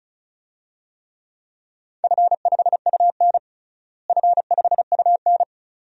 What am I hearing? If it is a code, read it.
Morse code, heard as a keyed tone, about 36 wpm, 700 Hz. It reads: F5UN F5UN
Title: DXCC-100 Call Signs - 4 Characters - Repeated 2x - 35wpm